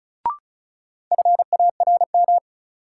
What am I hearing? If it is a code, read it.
Morse code, heard as a keyed tone, about 36 wpm, 700 Hz. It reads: FARM
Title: Sentences from Top 1000 Words - Review Part 2 - New Word Drill - 35wpm